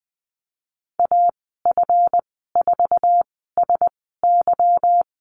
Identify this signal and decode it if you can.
Morse code, heard as a keyed tone, about 20 wpm, spaced as for 15 wpm, 700 Hz. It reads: AF4SY
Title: 193 Lesson 31g - Letter v - Call Signs 20wpm